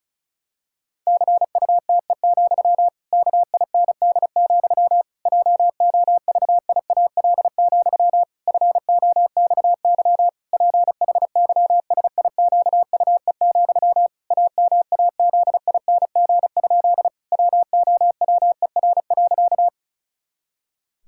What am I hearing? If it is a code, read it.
Morse code, heard as a keyed tone, about 34 wpm, 700 Hz. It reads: CUTE, KIND, JOVIAL, FOXY PHYSIQUE, AMAZING? WOWER.
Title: Pangram Challenge v2 - 35wpm